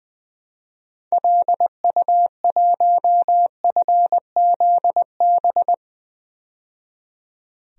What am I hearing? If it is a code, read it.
Morse code, heard as a keyed tone, about 20 wpm, 700 Hz. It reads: LU1FZB
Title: DXCC-100 Call Signs - 6 Characters - 20wpm